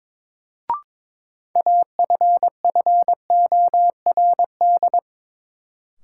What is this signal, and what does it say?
Morse code, heard as a keyed tone, about 22 wpm, 700 Hz. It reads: AFFORD
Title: Sentences from Top 1100 Words - New Word Drill - 22wpm